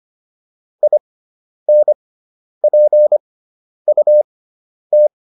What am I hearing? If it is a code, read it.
Morse code, heard as a keyed tone, about 25 wpm, spaced as for 10 wpm, 600 Hz. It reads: INPUT